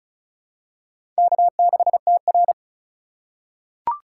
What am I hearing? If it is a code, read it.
Morse code, heard as a keyed tone, about 34 wpm, 700 Hz. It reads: K6TR
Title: USA Call Signs - 4x Character Spacing - 35wpm